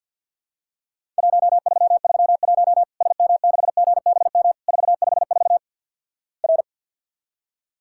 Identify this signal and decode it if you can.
Morse code, heard as a keyed tone, about 50 wpm, 700 Hz. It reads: 1331 SM6GBM 454 R
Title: Worked All Europe DX Contest - QTC Exchange with Repeats - 50wpm